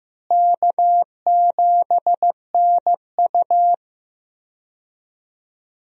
Morse code, heard as a keyed tone, about 15 wpm, 700 Hz. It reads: K7NU
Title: USA Call Signs - 4 Characters - Repeated 3x - 15wpm